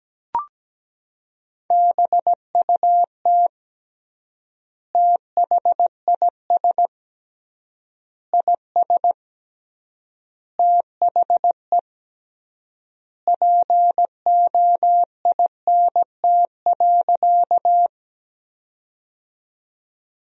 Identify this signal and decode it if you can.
Morse code, heard as a keyed tone, about 17 wpm, 700 Hz. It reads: BUT THIS IS THE POINT.
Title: 3x Word-Spacing - Sentences from Top 200 Words - 17wpm